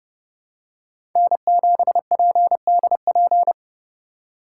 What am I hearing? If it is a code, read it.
Morse code, heard as a keyed tone, about 30 wpm, 700 Hz. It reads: N7PDP
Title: USA Call Signs - 2x Character Spacing - 30wpm